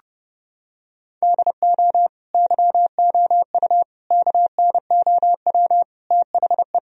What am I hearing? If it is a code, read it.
Morse code, heard as a keyed tone, about 30 wpm, 700 Hz. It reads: DO YOU KNOW THE
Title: Sentences from Top 1000 Words - Review Part 2 - 30wpm